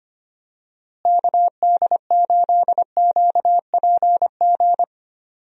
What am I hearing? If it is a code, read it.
Morse code, heard as a keyed tone, about 25 wpm, 700 Hz. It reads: KD8QPG
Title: USA Call Signs - 6 Characters - Repeated 3x - 25wpm